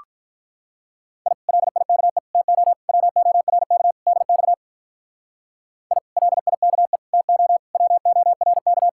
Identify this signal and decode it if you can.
Morse code, heard as a keyed tone, about 44 wpm, 700 Hz. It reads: I LIKE TO WORK DX I LIKE TO WORK
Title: Speed-Racing Common QSO Phrases - 35wpm